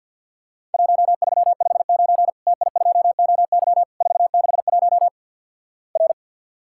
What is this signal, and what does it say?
Morse code, heard as a keyed tone, about 51 wpm, 700 Hz. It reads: 1359 TI2OY 461 R
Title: Worked All Europe DX Contest - QTC Exchange - 50wpm